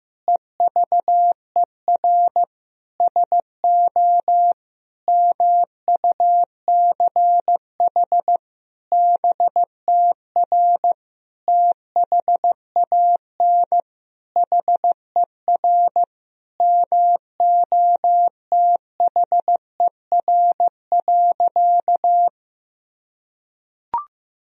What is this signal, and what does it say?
Morse code, heard as a keyed tone, about 15 wpm, 700 Hz. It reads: EVER SO MUCH BTR THAN HER MOTHER.
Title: Sentences from Top 400 Words and Abbreviations 15wpm